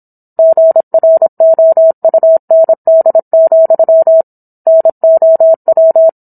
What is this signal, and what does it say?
Morse code, heard as a keyed tone, about 27 wpm, 650 Hz. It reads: GROUND, NOW